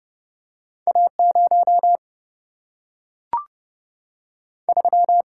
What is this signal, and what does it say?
Morse code, heard as a keyed tone, about 30 wpm, 700 Hz. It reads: A 0 3
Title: Single-Letter Single-Number 30wpm